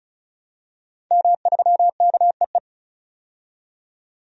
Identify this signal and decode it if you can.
Morse code, heard as a keyed tone, about 34 wpm, 700 Hz. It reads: M3KEE
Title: DXCC-100 Call Signs - 5 Characters - 35wpm